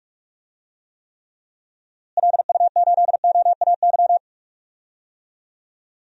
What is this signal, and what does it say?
Morse code, heard as a keyed tone, about 41 wpm, 700 Hz. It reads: LU8OAY
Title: DXCC-100 Call Signs - 6 Characters - Repeated 3x - 45wpm